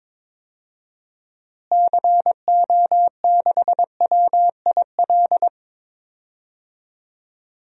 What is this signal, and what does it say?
Morse code, heard as a keyed tone, about 22 wpm, 700 Hz. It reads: CO6WIL
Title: DXCC-100 Call Signs - 6 Characters - Repeated 2x - 22wpm